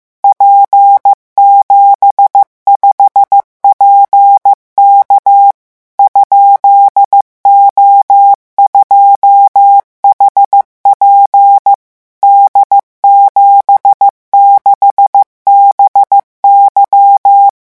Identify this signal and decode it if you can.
Morse code, heard as a keyed tone, about 15 wpm, 800 Hz. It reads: P75PK ?O2HP D76BY
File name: morse mp3 / revision13-vitesse 14 mots